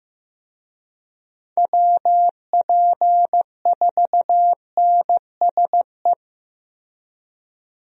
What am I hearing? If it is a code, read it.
Morse code, heard as a keyed tone, about 15 wpm, 700 Hz. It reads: WP4NSE
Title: USA Call Signs - 6 Characters - Repeated 2x - 15wpm